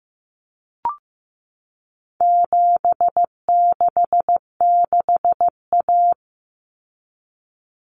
Morse code, heard as a keyed tone, about 15 wpm, 700 Hz. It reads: 766A
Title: Common Cut Numbers - 4-Digit 15wpm